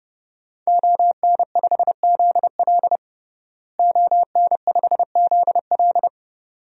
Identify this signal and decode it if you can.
Morse code, heard as a keyed tone, about 32 wpm, 700 Hz. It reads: ON5ZL ON5ZL
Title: DXCC-100 Call Signs - 5 Characters - Repeated 2x - 30wpm